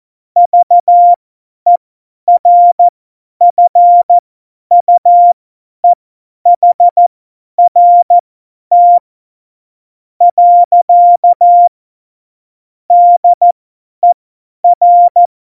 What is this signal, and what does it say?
Morse code, heard as a keyed tone, about 14 wpm, spaced as for 9 wpm, 700 Hz. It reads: VERFUEHRT. DER
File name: Morsekurs / Witze / Witz 08. Dezember 2020, 50 F70